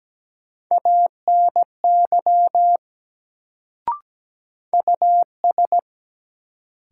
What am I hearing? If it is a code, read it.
Morse code, heard as a keyed tone, about 17 wpm, 700 Hz. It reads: ANY US
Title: Sentences from Top 200 Words - New Word Drill - 17wpm